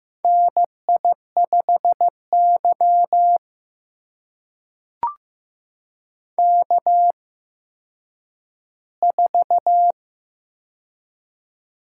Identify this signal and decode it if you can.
Morse code, heard as a keyed tone, about 15 wpm, spaced as for 9 wpm, 700 Hz. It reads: NI5Y K4
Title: USA Call Signs - 4 Characters - 8x Character Spacing - 15wpm